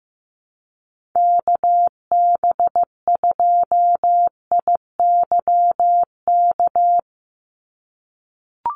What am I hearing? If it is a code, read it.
Morse code, heard as a keyed tone, about 15 wpm, 700 Hz. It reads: KB2IYK
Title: USA Call Signs 15wpm